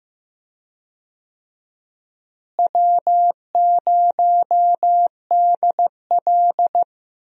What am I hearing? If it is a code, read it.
Morse code, heard as a keyed tone, about 15 wpm, 700 Hz. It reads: W0DL
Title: USA Call Signs - 4 Characters - Repeated 2x - 15wpm